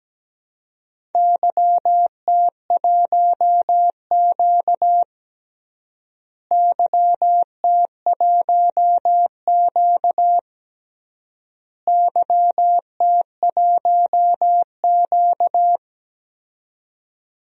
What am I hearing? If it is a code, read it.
Morse code, heard as a keyed tone, about 17 wpm, 700 Hz. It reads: YT1Q YT1Q YT1Q
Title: DXCC-100 Call Signs - 4 Characters - Repeated 3x - 17wpm